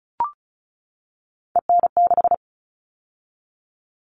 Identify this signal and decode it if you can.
Morse code, heard as a keyed tone, about 34 wpm, 700 Hz. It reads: EN6
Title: All Cut Numbers - 3-Digit 35wpm